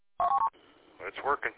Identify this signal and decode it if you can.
Speech. There is no code.